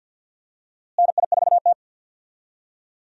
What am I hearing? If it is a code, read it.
Morse code, heard as a keyed tone, about 52 wpm, 700 Hz. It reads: NI4T
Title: USA Call Signs - 4 Characters - Repeated 2x - 50wpm